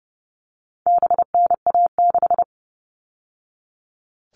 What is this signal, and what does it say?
Morse code, heard as a keyed tone, about 30 wpm, 700 Hz. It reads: BNA6
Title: All Cut Numbers - 4-Digit 30wpm